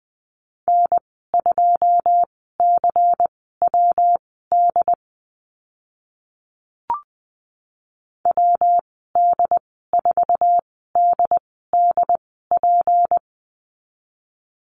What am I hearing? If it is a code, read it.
Morse code, heard as a keyed tone, about 20 wpm, spaced as for 14 wpm, 700 Hz. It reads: N2CWD WD4DDP
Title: 161 Lesson 26g - Letter g - Call Signs 20wpm